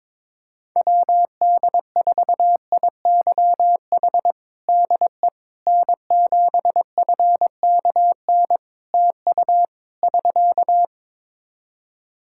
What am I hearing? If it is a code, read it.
Morse code, heard as a keyed tone, about 21 wpm, 700 Hz. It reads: WD4IYH DE N7FKN TU <SK>
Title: Practice QSO - Exchange 4 - Ending - 22wpm